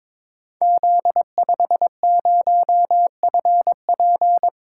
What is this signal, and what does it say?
Morse code, heard as a keyed tone, about 23 wpm, 700 Hz. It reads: Z50FP
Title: Code Groups - 5 Letter:Numbers - 22wpm